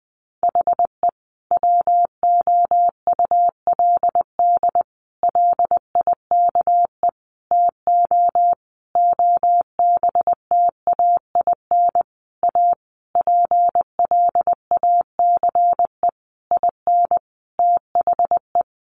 Morse code, heard as a keyed tone, about 20 wpm, 700 Hz. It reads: HE WOULD LIKE TO OBTAIN A PLACE IN THE
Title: Sentences from Top 1300 Words 20wpm